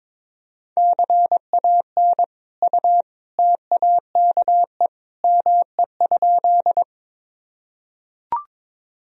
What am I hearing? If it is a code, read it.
Morse code, heard as a keyed tone, about 22 wpm, 700 Hz. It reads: CAN U TAKE ME?
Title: Sentences from Top 200 Words and Abbreviations 22wpm